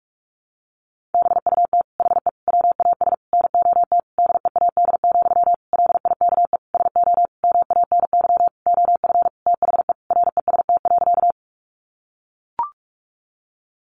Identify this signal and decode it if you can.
Morse code, heard as a keyed tone, about 44 wpm, 700 Hz. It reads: BUT HE WAS NOT DEAD, LIKE SO MANY OF THE REST.
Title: Sentences from Top 1500 Words 45wpm